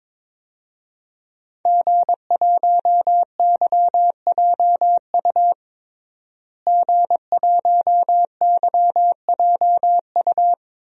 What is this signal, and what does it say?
Morse code, heard as a keyed tone, about 22 wpm, 700 Hz. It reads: G1YJU G1YJU
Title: DXCC-100 Call Signs - 5 Characters - Repeated 3x - 22wpm